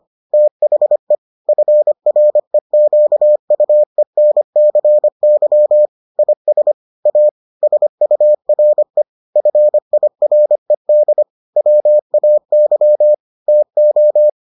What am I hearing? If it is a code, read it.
Morse code, heard as a keyed tone, about 25 wpm, 600 Hz. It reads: THE FREQUENCY IS A SURE FIRED WAY TO